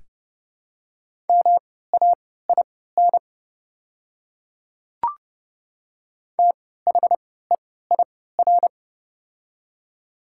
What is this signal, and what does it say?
Morse code, heard as a keyed tone, about 30 wpm, spaced as for 17 wpm, 700 Hz. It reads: MAIN THEIR
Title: 3x Character Spacing - Top 300 Words - 30wpm